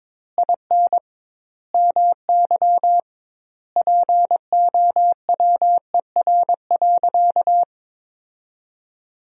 Morse code, heard as a keyed tone, about 22 wpm, 700 Hz. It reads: IN MY POWER.
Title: 2x Word-Spacing - Sentences from Top 500 Words - 22wpm